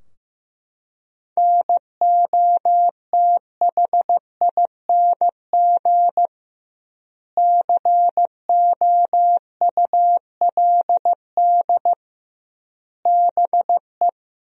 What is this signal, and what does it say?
Morse code, heard as a keyed tone, about 15 wpm, 700 Hz. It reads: NOTHING COULD BE
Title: 2x Word-Spacing - Sentences from Top 500 Words - 15wpm